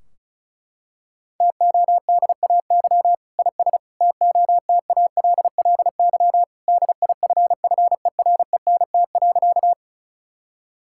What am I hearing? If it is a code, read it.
Morse code, heard as a keyed tone, about 36 wpm, 700 Hz. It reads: TODAY IS TOTALLY DIFFERENT.
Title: Sentences from Top 1600 Words 35wpm